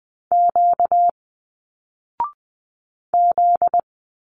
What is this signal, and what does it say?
Morse code, heard as a keyed tone, about 20 wpm, 700 Hz. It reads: Q Z
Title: Single Letters 20wpm